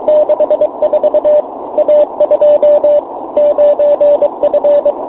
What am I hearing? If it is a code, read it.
Morse code, heard as a keyed tone, about 23 wpm, 600 Hz, with radio noise. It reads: 64 A2 9F
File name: fo-29 cw bake